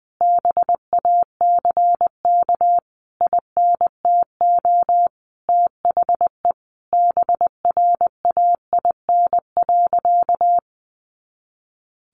Morse code, heard as a keyed tone, about 20 wpm, 700 Hz. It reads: BACK INTO THE BRAIN.
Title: A Sound Of Thunder 20wpm